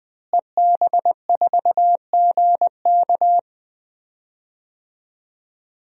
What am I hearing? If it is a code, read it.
Morse code, heard as a keyed tone, about 20 wpm, 700 Hz. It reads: EB4GK
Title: DXCC-100 Call Signs - 5 Characters - Repeated 2x - 20wpm